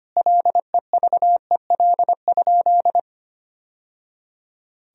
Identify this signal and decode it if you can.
Morse code, heard as a keyed tone, about 25 wpm, 700 Hz. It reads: LEVEL?
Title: Sentences from Top 1000 Words - Review Part 4 - 25wpm